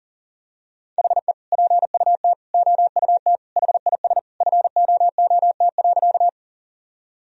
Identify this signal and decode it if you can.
Morse code, heard as a keyed tone, about 40 wpm, 700 Hz. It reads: HE PUT OUT HIS FOOT.
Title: Sentences from Top 1000 Words - Review Part 3 - 40wpm